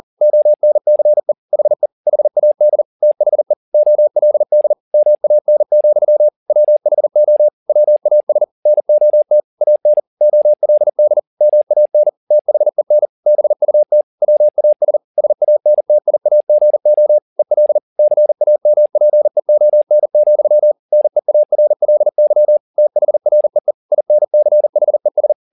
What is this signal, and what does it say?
Morse code, heard as a keyed tone, about 40 wpm, 600 Hz. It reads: ONCE HE HAD THE OLD MAN, WHO WAS NOT AN OLD MAN THEN BUT WAS SANTIAGO EL CAMPEON, NEARLY THREE INCHES
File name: the old man and the sea 2 40wpm